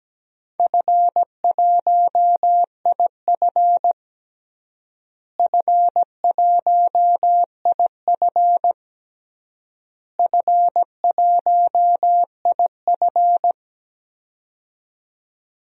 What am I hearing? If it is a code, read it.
Morse code, heard as a keyed tone, about 17 wpm, 700 Hz. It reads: F1IF F1IF F1IF
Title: DXCC-100 Call Signs - 4 Characters - Repeated 3x - 17wpm